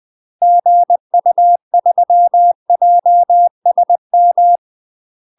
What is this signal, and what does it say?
Morse code, heard as a keyed tone, about 20 wpm, 700 Hz. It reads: GU3JSM
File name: UK MASTER QSOs TRAINING / 20WPM / UK MASTER QSOs 0032